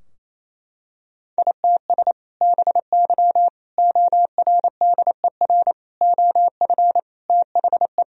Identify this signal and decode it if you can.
Morse code, heard as a keyed tone, about 27 wpm, 700 Hz. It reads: ITS BY ORDER OF THE
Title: Sentences from Top 1000 Words - Review Part 3 - 28wpm